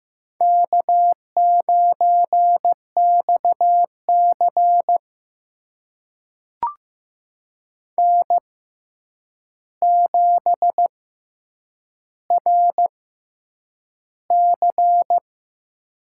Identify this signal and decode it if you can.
Morse code, heard as a keyed tone, about 15 wpm, spaced as for 8 wpm, 700 Hz. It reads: K9XC N7RC
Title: USA Call Signs - 4 Characters - 6x Character Spacing - 15wpm